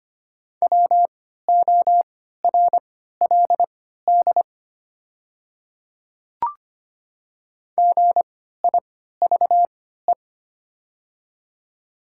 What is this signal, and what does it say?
Morse code, heard as a keyed tone, about 25 wpm, spaced as for 14 wpm, 700 Hz. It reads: WORLD GIVE
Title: 3x Character Spacing - Top 200 Words - 25wpm